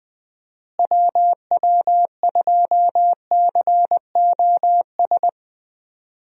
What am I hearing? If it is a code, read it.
Morse code, heard as a keyed tone, about 20 wpm, 700 Hz. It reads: WW2COS
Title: USA Call Signs - 6 Characters - Repeated 3x - 20wpm